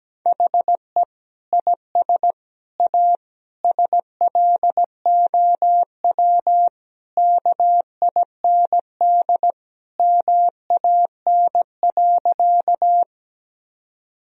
Morse code, heard as a keyed tone, about 17 wpm, 700 Hz. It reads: HE IS A SLOW KIND MAN.
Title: Sentences from Top 1000 Words - Review Part 2 - 17wpm